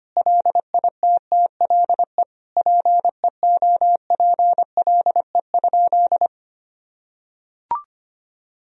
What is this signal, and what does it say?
Morse code, heard as a keyed tone, about 25 wpm, 700 Hz. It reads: LITTLE PEOPLE?
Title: Sentences from Top 1000 Words - Review Part 1 - 25wpm